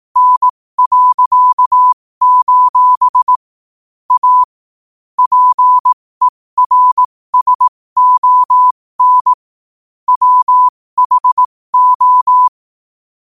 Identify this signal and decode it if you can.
Morse code, heard as a keyed tone, about 18 wpm, spaced as for 15 wpm, 1000 Hz. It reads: N.: A PERSON WHO
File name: qotdfiles / 15wpm